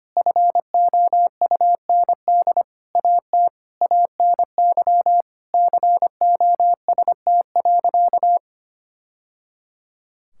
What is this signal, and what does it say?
Morse code, heard as a keyed tone, about 25 wpm, 700 Hz. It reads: FOUND AT ANY COST.
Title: Sentences from Top 1000 Words - Review Part 5 - 25wpm